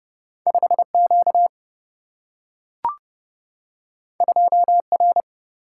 Morse code, heard as a keyed tone, about 30 wpm, 700 Hz. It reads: 5 Q 2 R
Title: Single-Letter Single-Number 30wpm